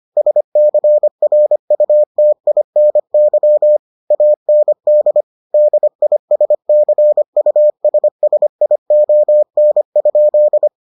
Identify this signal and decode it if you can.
Morse code, heard as a keyed tone, about 25 wpm, 600 Hz. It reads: SCRUTINY AND DISCUSSION?